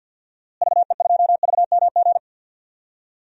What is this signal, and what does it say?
Morse code, heard as a keyed tone, about 52 wpm, 700 Hz. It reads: VE2VMG
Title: DXCC-100 Call Signs - 6 Characters - Repeated 3x - 50wpm